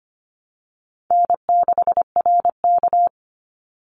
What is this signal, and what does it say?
Morse code, heard as a keyed tone, about 25 wpm, 700 Hz. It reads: N6RK
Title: Call Signs - 25wpm